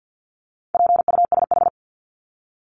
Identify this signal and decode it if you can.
Morse code, heard as a keyed tone, about 50 wpm, 700 Hz. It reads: PUSH
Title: Top 1001-1250 Words 50wpm